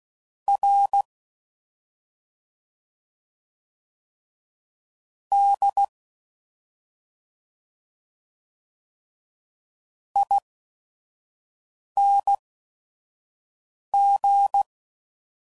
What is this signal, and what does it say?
Morse code, heard as a keyed tone, about 16 wpm, spaced as for 5 wpm, 800 Hz. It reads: R D ING
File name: Suffixes / Special ING's / Sngl ING's